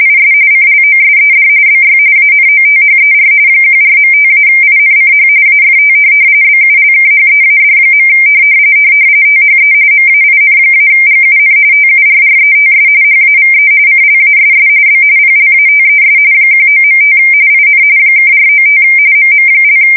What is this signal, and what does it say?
RTTY, about 45.45 baud. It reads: IS CLOSED BY ANOTHER 14.5-INCH AFT BULKHEAD. THE MAIN-GUN TURRETS HAVE CLASS B PLATES 19.5 IN (495 MM)
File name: fsk170 / USS-Missouri